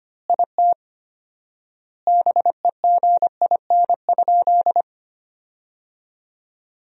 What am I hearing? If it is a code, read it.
Morse code, heard as a keyed tone, about 25 wpm, 700 Hz. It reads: IT BEGIN?
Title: 4x Word-Spacing - Sentences from Top 300 Words - 25wpm